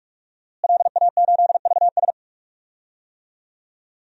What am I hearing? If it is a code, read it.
Morse code, heard as a keyed tone, about 46 wpm, 700 Hz. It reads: LA8VS
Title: DXCC-100 Call Signs - 5 Characters - Repeated 3x - 45wpm